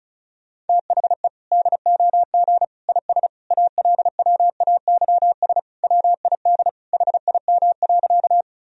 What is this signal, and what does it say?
Morse code, heard as a keyed tone, about 36 wpm, 700 Hz. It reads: THE DOG IS ALWAYS WID HIM.
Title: Sentences from Top 400 Words and Abbreviations 35wpm